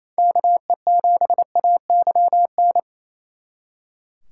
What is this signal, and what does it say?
Morse code, heard as a keyed tone, about 36 wpm, 700 Hz. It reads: KE7AYN